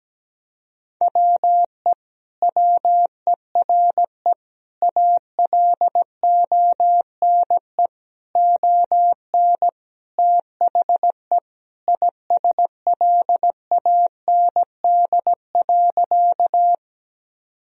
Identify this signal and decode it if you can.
Morse code, heard as a keyed tone, about 17 wpm, 700 Hz. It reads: WE WERE ALONE ON THE ISLAND.
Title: Sentences from Top 1000 Words - Review Part 3 - 17wpm